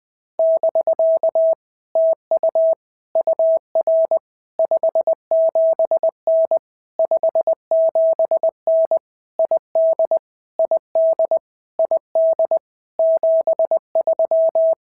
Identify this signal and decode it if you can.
Morse code, heard as a keyed tone, about 20 wpm, 650 Hz. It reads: <BK> TU UR 57N 57N ID ID ID 73
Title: POTA QSO Practice - 20wpm